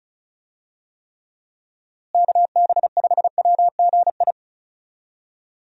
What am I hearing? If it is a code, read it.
Morse code, heard as a keyed tone, about 36 wpm, 700 Hz. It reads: KB5WGI
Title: USA Call Signs - 6 Characters - Repeated 3x - 35wpm